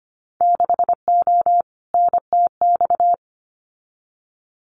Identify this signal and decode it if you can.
Morse code, heard as a keyed tone, about 25 wpm, 700 Hz. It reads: 6O NTX
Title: Winter Field Day Exchanges Encore 25wpm